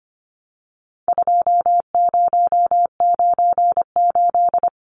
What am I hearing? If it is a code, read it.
Morse code, heard as a keyed tone, about 25 wpm, 700 Hz. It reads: 2098
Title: Numbers - 4-Digit 25wpm